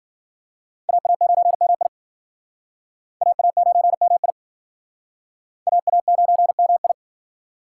Morse code, heard as a keyed tone, about 50 wpm, 700 Hz. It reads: AA9MI AA9MI AA9MI
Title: Speed Racing - USA 2x2 Call Signs 40wpm